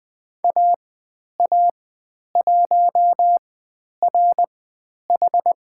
Morse code, heard as a keyed tone, about 20 wpm, spaced as for 10 wpm, 700 Hz. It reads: AA1RH
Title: 102 Lesson 17g - Letter u - Call Signs 20wpm